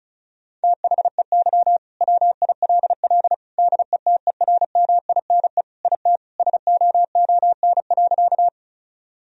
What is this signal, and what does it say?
Morse code, heard as a keyed tone, about 36 wpm, 700 Hz. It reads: THEY WILL DETERMINE IT SOON.
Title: Sentences from Top 1000 Words - Review Part 4 - 35wpm